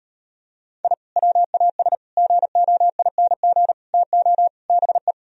Morse code, heard as a keyed tone, about 39 wpm, 700 Hz. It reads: I WAS GOING TO BE